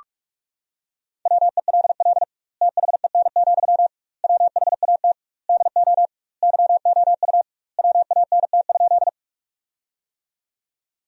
Morse code, heard as a keyed tone, about 46 wpm, 700 Hz. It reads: WELL THEN, WHAT DO YOU WANT?
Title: 2x Word-Spacing - Sentences from Top 200 Words - 45wpm